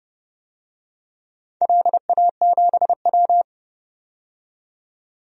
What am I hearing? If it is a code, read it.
Morse code, heard as a keyed tone, about 30 wpm, 700 Hz. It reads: LA7W
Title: DXCC-100 Call Signs - 4 Characters - 30wpm